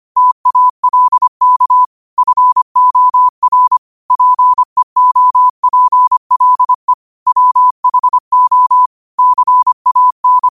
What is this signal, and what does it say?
Morse code, heard as a keyed tone, about 25 wpm, 1000 Hz. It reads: TALK FOR PEOPLE WHO CAN